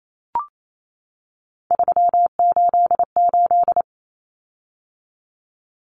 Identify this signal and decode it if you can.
Morse code, heard as a keyed tone, about 28 wpm, 700 Hz. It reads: 388
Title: Common Cut Numbers - 3-Digit 28wpm